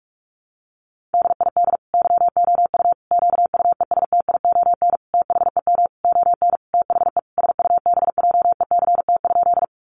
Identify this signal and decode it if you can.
Morse code, heard as a keyed tone, about 44 wpm, 700 Hz. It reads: DID YOU QUESTION THEM ON THE SUBJECT?
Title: Sentences from Top 800 Words 45wpm